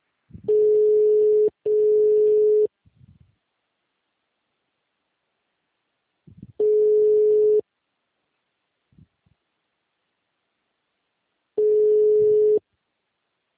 Ringback tone (Europe, 425 Hz).